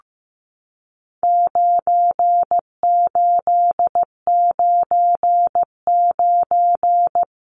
Morse code, heard as a keyed tone, about 15 wpm, 700 Hz. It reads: 9899
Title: Numbers - 4-Digit 15wpm